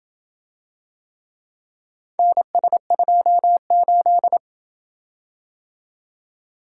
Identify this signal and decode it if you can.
Morse code, heard as a keyed tone, about 26 wpm, 700 Hz. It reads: NS28